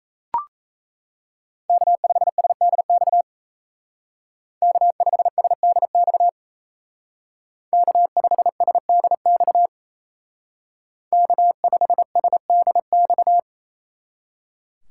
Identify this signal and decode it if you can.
Morse code, heard as a keyed tone, about 36 wpm, 700 Hz. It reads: K5SDX K5SDX K5SDX K5SDX